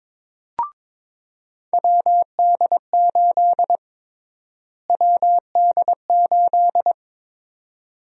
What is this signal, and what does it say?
Morse code, heard as a keyed tone, about 22 wpm, 700 Hz. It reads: WD8 WD8
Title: USA Call Sign Prefixes - Repeated 2x - 22wpm